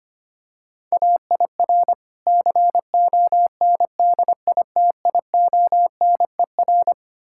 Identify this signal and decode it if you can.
Morse code, heard as a keyed tone, about 26 wpm, 700 Hz. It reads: AIR CONDITIONER
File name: practice / 25wpm / Difficult Words - Speed-Racing with Hints - 25wpm